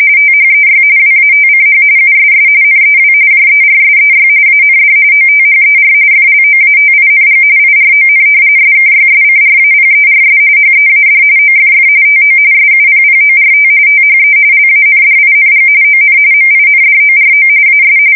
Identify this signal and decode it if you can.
RTTY, about 45.45 baud. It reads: THEY WATCH USS NEW JERSEY AND OTHER FLOATING IRON SKYSCRAPERS HEAD OFF TO SUPPORT THE LANDING ON MINDORO AS
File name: fsk170 / battleshipnj